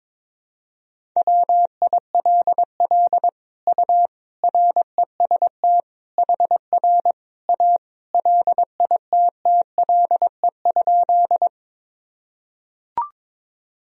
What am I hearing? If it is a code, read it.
Morse code, heard as a keyed tone, about 22 wpm, 700 Hz. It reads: WILL U REST HR A LITTLE?
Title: Sentences from Top 500 Words and Abbreviations 22wpm